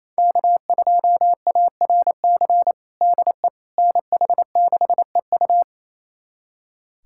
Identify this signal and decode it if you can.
Morse code, heard as a keyed tone, about 27 wpm, 700 Hz. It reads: K2ARC DE NH6EU
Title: USA Callsign DE Callsign - 5 Characters - 28wpm